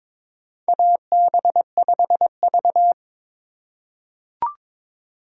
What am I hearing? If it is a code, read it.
Morse code, heard as a keyed tone, about 22 wpm, 700 Hz. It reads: AB5V
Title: USA Call Signs - 4 Characters - 2x Character Spacing - 22wpm